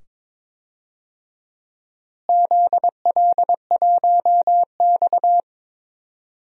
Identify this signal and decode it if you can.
Morse code, heard as a keyed tone, about 22 wpm, 700 Hz. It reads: ZL1X
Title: DXCC-100 Call Signs - 4 Characters - Repeated 2x - 22wpm